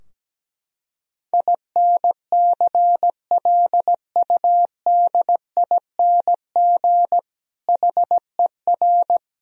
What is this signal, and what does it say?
Morse code, heard as a keyed tone, about 17 wpm, 700 Hz. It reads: INCLUDING HER
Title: Sentences from Top 1000 Words - Review Part 4 - 17wpm